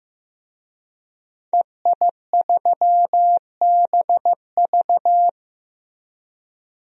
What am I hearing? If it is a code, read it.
Morse code, heard as a keyed tone, about 15 wpm, 700 Hz. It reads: EI3BV
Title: DXCC-100 Call Signs - 5 Characters - Repeated 3x - 15wpm